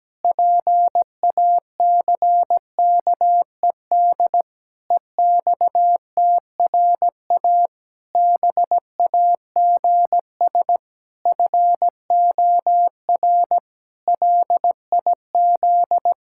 Morse code, heard as a keyed tone, about 17 wpm, 700 Hz. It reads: PACKED EXTRA BAGS FOR LIZ
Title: Pangram Challenge v2 - 17wpm